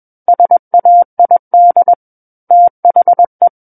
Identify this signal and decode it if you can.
Morse code, heard as a keyed tone, about 21 wpm, 700 Hz. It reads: SAID THE
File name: Book One / 12-Chapter 12 - 21 WPM